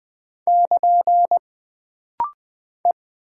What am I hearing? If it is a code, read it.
Morse code, heard as a keyed tone, about 20 wpm, 700 Hz. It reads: <KN> E
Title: Single Character (Letter:Number:Punctuation:Prosign) - Rapid-Fire - 20wpm